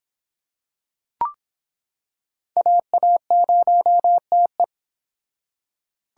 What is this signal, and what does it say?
Morse code, heard as a keyed tone, about 26 wpm, 700 Hz. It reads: AA0TE